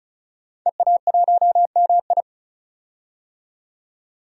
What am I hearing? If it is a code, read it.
Morse code, heard as a keyed tone, about 36 wpm, 700 Hz. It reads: EA1MI
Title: DXCC-100 Call Signs - 5 Characters - 35wpm